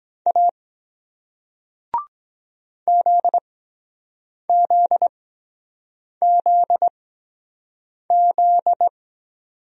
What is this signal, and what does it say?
Morse code, heard as a keyed tone, about 23 wpm, spaced as for 7 wpm, 700 Hz. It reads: A ZZZZ